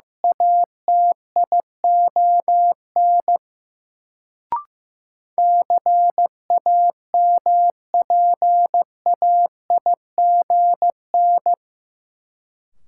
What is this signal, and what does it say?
Morse code, heard as a keyed tone, about 15 wpm, 700 Hz. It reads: ATION CAMPAIGN
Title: Sentences from Top 1200 Words - New Word Drill Alt - 15wpm